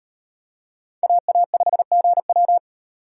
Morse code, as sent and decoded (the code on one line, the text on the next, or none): .- .- ..... --. .--
AA5GW